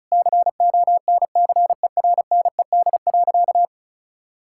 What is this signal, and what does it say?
Morse code, heard as a keyed tone, about 36 wpm, 700 Hz. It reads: CONCERNED.
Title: Sentences from Top 1000 Words - Review Part 6 - 35wpm